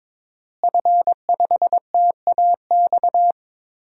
Morse code, heard as a keyed tone, about 21 wpm, 700 Hz. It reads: F5TAX
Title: DXCC-100 Call Signs - 5 Characters - Repeated 2x - 22wpm